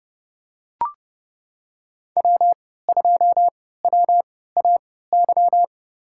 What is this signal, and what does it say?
Morse code, heard as a keyed tone, about 29 wpm, spaced as for 17 wpm, 700 Hz. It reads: W2WAY
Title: USA Call Signs - 3x Character Spacing - 30wpm